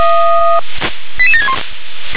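Five-tone selective call (ZVEI), heard as quick runs of five tones, tones of about 70 ms. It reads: 88761
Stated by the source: ZVEI five-tone call 88722 received on SDR.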